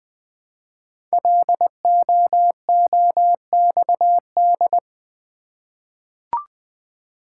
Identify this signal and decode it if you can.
Morse code, heard as a keyed tone, about 20 wpm, 700 Hz. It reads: LOOXD
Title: Code Groups - 5 Letter:Numbers - 20wpm